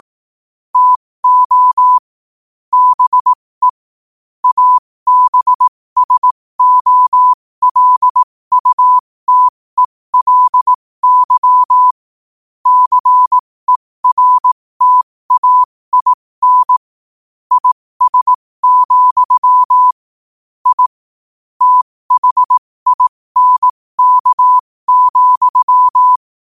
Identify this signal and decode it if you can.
Morse code, heard as a keyed tone, about 18 wpm, spaced as for 15 wpm, 1000 Hz. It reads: TO BE ABSOLUTELY CERTAIN IS, I THINK,
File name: qotdfiles / 15wpm